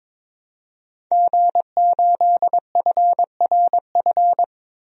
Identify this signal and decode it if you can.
Morse code, heard as a keyed tone, about 22 wpm, 700 Hz. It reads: G8FRF
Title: DXCC-100 Call Signs - 5 Characters - 22wpm